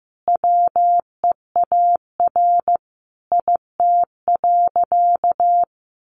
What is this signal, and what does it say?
Morse code, heard as a keyed tone, about 15 wpm, 700 Hz. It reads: WEAR IT.
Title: Sentences from Top 1500 Words 15wpm